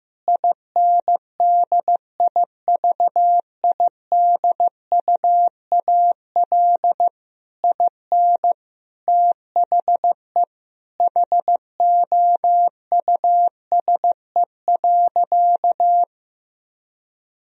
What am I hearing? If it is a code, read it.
Morse code, heard as a keyed tone, about 15 wpm, 700 Hz. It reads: INDIVIDUAL IN THE HOUSE.
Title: Sentences from Top 1000 Words - Review Part 4 - 15wpm